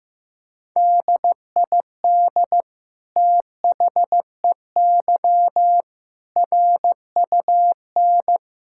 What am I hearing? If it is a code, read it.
Morse code, heard as a keyed tone, about 16 wpm, 700 Hz. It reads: DID THEY RUN
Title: Sentences from Top 1000 Words - Review Part 5 - 15wpm